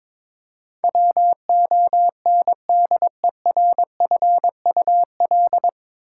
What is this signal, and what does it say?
Morse code, heard as a keyed tone, about 22 wpm, 700 Hz. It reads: WONDERFUL
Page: Morse Code Ninja practice 22wpm